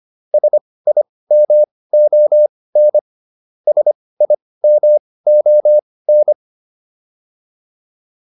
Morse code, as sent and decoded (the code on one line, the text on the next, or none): ... .. -- --- -. / ... .. -- --- -.
SIMON SIMON